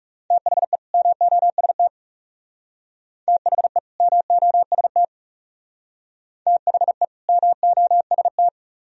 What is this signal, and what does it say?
Morse code, heard as a keyed tone, about 36 wpm, 700 Hz. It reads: THE MOST THE MOST THE MOST
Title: Speed-Racing - Sets of 2 Words 30wpm